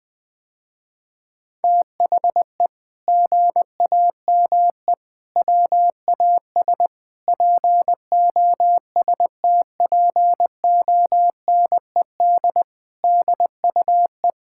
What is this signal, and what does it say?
Morse code, heard as a keyed tone, about 21 wpm, 700 Hz. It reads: THE GAME WAS POSTPONED DUE
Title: Sentences with 6-Word Phrases - Spoken Beginning - 20wpm